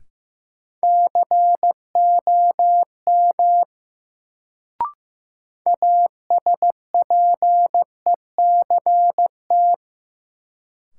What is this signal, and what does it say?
Morse code, heard as a keyed tone, about 15 wpm, 700 Hz. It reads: COM ASPECT
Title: Sentences from Top 1300 Words - New Word Drill Alt - 15wpm